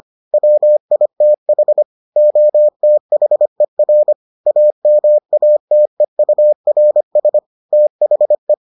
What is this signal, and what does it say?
Morse code, heard as a keyed tone, about 25 wpm, 600 Hz. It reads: WITH OTHER AMATEURS THE